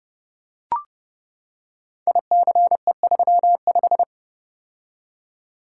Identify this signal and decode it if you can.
Morse code, heard as a keyed tone, about 30 wpm, 700 Hz. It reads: ICE35
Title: Code Groups - 5 Letter:Numbers - 30wpm